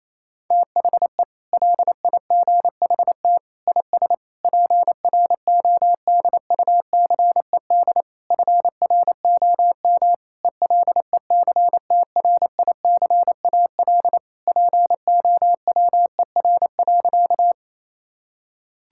Morse code, heard as a keyed tone, about 28 wpm, 700 Hz. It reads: THE LIGHT IS PRODUCED FROM ELECTRICAL POWER.
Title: Sentences from Top 1000 Words - Review Part 6 - 28wpm